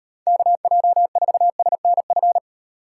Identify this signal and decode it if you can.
Morse code, heard as a keyed tone, about 38 wpm, 700 Hz. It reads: KJ4SNF